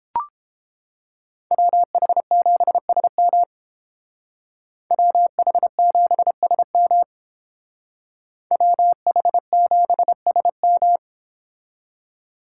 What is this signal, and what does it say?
Morse code, heard as a keyed tone, about 31 wpm, 700 Hz. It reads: WH7SM WH7SM WH7SM